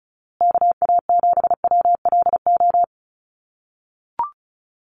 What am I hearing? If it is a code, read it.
Morse code, heard as a keyed tone, about 34 wpm, 700 Hz. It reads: KA7WLO
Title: USA Call Signs Repeated 2x 35wpm